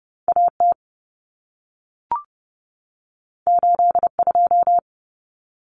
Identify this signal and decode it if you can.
Morse code, heard as a keyed tone, about 30 wpm, 700 Hz. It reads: AT 82
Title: Common Cut Numbers - 2-Digit 30wpm